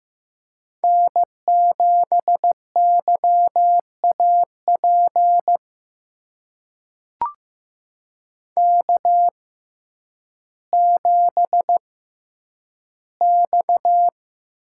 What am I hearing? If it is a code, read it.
Morse code, heard as a keyed tone, about 15 wpm, spaced as for 9 wpm, 700 Hz. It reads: N7YAP K7X
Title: USA Call Signs - 6x Character Spacing - 15wpm